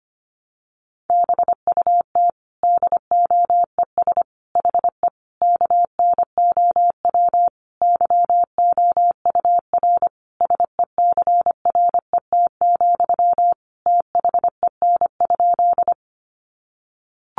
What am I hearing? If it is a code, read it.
Morse code, heard as a keyed tone, about 25 wpm, 700 Hz. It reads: BUT DOES HE KNOW YOUR SECRET, THEN?
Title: Sentences from Top 1200 Words 25wpm